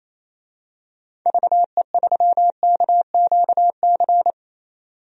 Morse code, heard as a keyed tone, about 28 wpm, 700 Hz. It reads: VE3KQC
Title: DXCC-100 Call Signs - 6 Characters - 28wpm